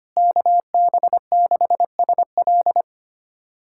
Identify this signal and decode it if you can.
Morse code, heard as a keyed tone, about 25 wpm, 700 Hz. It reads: KB6SL
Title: USA Callsign DE Callsign 25wpm